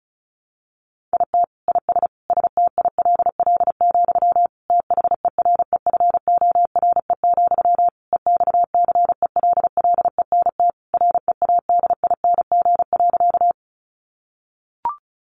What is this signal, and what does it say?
Morse code, heard as a keyed tone, about 34 wpm, 700 Hz. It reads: IT IS STILL, THEREFORE, EXCELLENT READING.
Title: Sentences from Top 1000 Words 35wpm